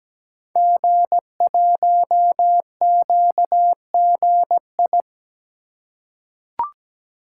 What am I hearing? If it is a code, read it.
Morse code, heard as a keyed tone, about 17 wpm, 700 Hz. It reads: G1QGI
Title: Code Groups - 5 Letter:Numbers - 17wpm